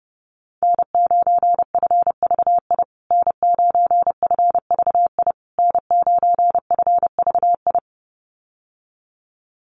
Morse code, heard as a keyed tone, about 30 wpm, 700 Hz. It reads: N9FVI N9FVI N9FVI
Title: USA Call Signs Repeated 3x 30wpm